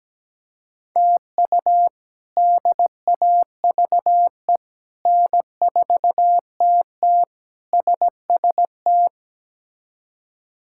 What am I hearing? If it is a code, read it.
Morse code, heard as a keyed tone, about 17 wpm, 700 Hz. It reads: TU DAVE N4TT SST
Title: K1USN SST QSO Practice - 17wpm